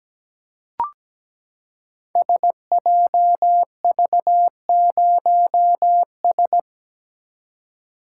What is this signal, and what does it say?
Morse code, heard as a keyed tone, about 17 wpm, 700 Hz. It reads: SJV0S
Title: Code Groups - 5 Letter:Numbers - 17wpm